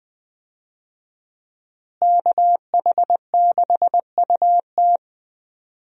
Morse code, heard as a keyed tone, about 20 wpm, 700 Hz. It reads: KH6UT
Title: USA Call Signs - 5 Characters - Repeated 2x - 20wpm